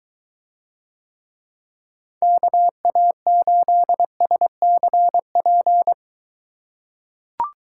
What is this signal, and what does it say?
Morse code, heard as a keyed tone, about 22 wpm, 700 Hz. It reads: KA8SCP